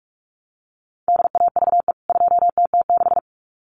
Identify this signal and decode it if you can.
Morse code, heard as a keyed tone, about 44 wpm, 700 Hz. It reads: DAVE 2TT6
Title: CWT Exchange 45wpm